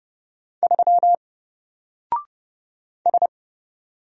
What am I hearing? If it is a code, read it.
Morse code, heard as a keyed tone, about 30 wpm, 700 Hz. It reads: 3 S
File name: Single-Letter-Number-30wpm